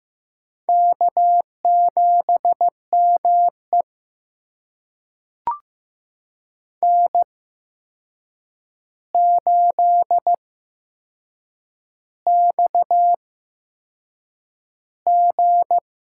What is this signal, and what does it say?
Morse code, heard as a keyed tone, about 15 wpm, spaced as for 6 wpm, 700 Hz. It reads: K7ME N8XG
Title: USA Call Signs - 4 Characters - 8x Character Spacing - 15wpm